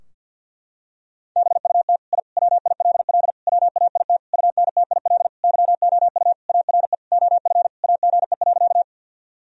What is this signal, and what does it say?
Morse code, heard as a keyed tone, about 46 wpm, 700 Hz. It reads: BUT I WILL WAIT UNTIL YOU ARE OF AGE.
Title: Sentences from Top 1000 Words - Review Part 3 - 50wpm